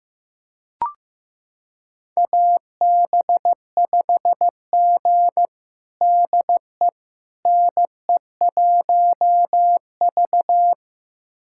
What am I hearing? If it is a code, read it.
Morse code, heard as a keyed tone, about 15 wpm, 700 Hz. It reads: AB5G DE NE1V
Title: USA Callsign DE Callsign - 4 Characters - 15wpm